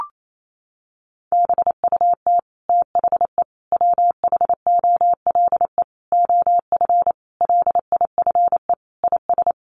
Morse code, heard as a keyed tone, about 28 wpm, 700 Hz. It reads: BUT THE WHOLE OF LIFE IS
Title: Sentences from Top 1500 Words 28wpm